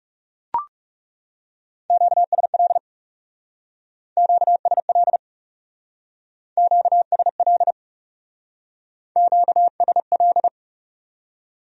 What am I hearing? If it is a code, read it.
Morse code, heard as a keyed tone, about 39 wpm, 700 Hz. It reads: QSL QSL QSL QSL